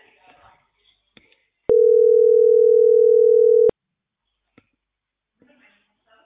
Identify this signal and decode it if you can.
Ringback tone (North America), 440 + 480 Hz.